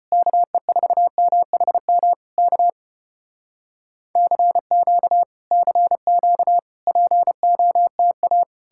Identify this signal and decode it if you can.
Morse code, heard as a keyed tone, about 35 wpm, 700 Hz. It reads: KE4MHM K CQ CQ POTA